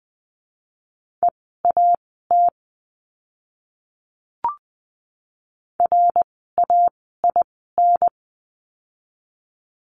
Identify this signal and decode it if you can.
Morse code, heard as a keyed tone, about 20 wpm, spaced as for 15 wpm, 700 Hz. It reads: EAT RAIN
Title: 068 Lesson 12f - Letter d - Words 20wpm